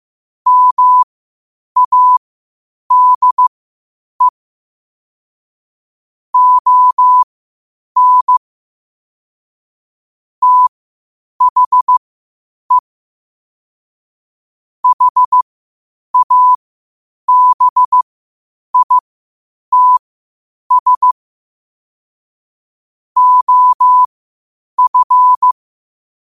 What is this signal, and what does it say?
Morse code, heard as a keyed tone, about 15 wpm, spaced as for 8 wpm, 1000 Hz. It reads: MADE ON THE HABITS OF
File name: qotdfiles / 8wpm